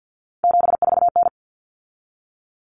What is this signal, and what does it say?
Morse code, heard as a keyed tone, about 50 wpm, 700 Hz. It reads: 74N
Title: Common Cut Numbers - 3-Digit 50wpm